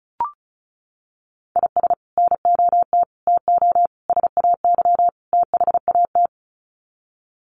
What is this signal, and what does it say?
Morse code, heard as a keyed tone, about 34 wpm, 700 Hz. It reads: IS NOT TO SAY THAT
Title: Sets of 5 Words 35wpm